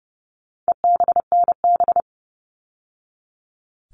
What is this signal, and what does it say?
Morse code, heard as a keyed tone, about 30 wpm, 700 Hz. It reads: EBNB
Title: All Cut Numbers - 4-Digit 30wpm